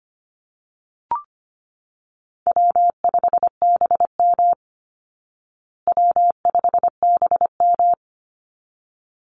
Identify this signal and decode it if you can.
Morse code, heard as a keyed tone, about 25 wpm, 700 Hz. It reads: W5BM W5BM
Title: Easy Call Signs Repeated 2x - 25wpm